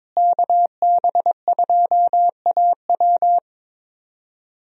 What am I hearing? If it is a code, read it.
Morse code, heard as a keyed tone, about 22 wpm, 700 Hz. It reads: KB2AW
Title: USA Call Signs - 5 Characters - Repeated 3x - 22wpm